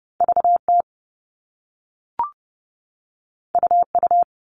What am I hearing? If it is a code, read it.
Morse code, heard as a keyed tone, about 30 wpm, 700 Hz. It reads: VT UU
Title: All Cut Numbers - 2-Digit 30wpm